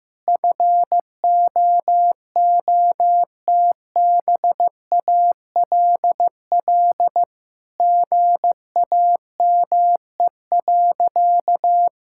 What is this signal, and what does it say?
Morse code, heard as a keyed tone, about 16 wpm, 700 Hz. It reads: FOOTBALL GAME.
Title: Sentences from Top 1600 Words 15wpm